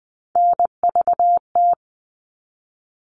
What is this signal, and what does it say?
Morse code, heard as a keyed tone, about 20 wpm, 700 Hz. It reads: NVT
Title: All Cut Numbers - 3-Digit 20wpm